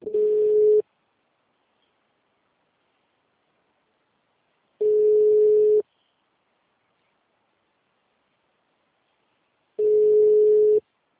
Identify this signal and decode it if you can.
Ringback tone (Europe, 425 Hz).
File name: out- -2010- - - .55680